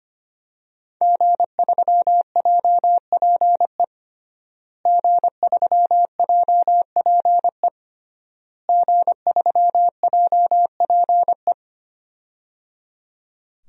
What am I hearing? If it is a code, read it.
Morse code, heard as a keyed tone, about 25 wpm, 700 Hz. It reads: G3JPE G3JPE G3JPE
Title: DXCC-100 Call Signs - 5 Characters - Repeated 3x - 25wpm